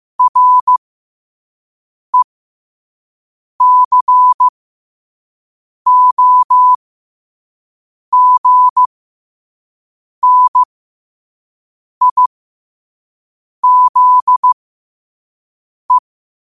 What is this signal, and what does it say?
Morse code, heard as a keyed tone, about 15 wpm, spaced as for 5 wpm, 1000 Hz. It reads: RECOGNIZE
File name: 5wpm